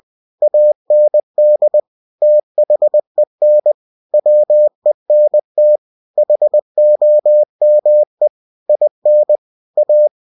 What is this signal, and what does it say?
Morse code, heard as a keyed tone, about 20 wpm, 600 Hz. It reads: AND THEN WENT HOME IN A